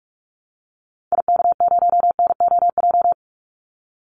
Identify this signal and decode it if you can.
Morse code, heard as a keyed tone, about 44 wpm, 700 Hz. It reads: IK0NOJ
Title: Call Signs Repeated 2x - 45wpm